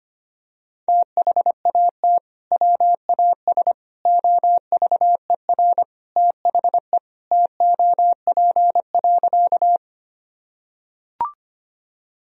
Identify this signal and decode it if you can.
Morse code, heard as a keyed tone, about 25 wpm, 700 Hz. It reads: THAT WAS OVER THE TOP.
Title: Sentences from Top 1000 Words - Review Part 2 - 25wpm